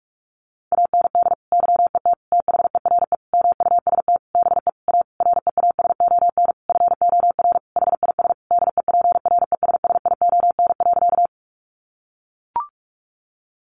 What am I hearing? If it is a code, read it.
Morse code, heard as a keyed tone, about 44 wpm, 700 Hz. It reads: AND YET THERE MUST BE A REASON FOR HIS DEPRESSION.
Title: Sentences from Top 1300 Words 45wpm